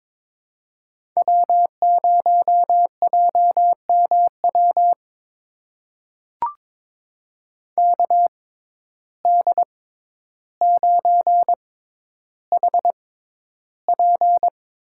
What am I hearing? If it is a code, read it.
Morse code, heard as a keyed tone, about 22 wpm, spaced as for 11 wpm, 700 Hz. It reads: W0JMW KD9HP
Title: USA Call Signs - 6x Character Spacing - 22wpm